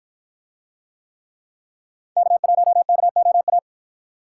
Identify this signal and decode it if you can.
Morse code, heard as a keyed tone, about 53 wpm, 700 Hz. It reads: K1KOA